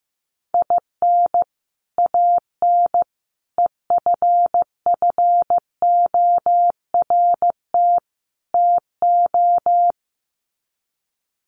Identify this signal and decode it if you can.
Morse code, heard as a keyed tone, about 15 wpm, 700 Hz. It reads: IN AN EFFORT TO
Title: Sets of 4 Words 15wpm